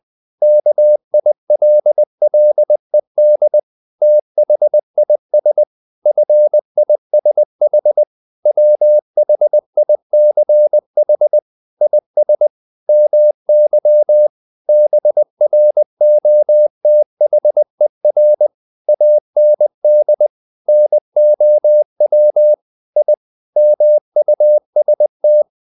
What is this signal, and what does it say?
Morse code, heard as a keyed tone, about 20 wpm, 600 Hz. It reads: KILLED THIS FISH WHICH IS MY BROTHER AND NOW I MUST
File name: the old man and the sea part 2 20wpm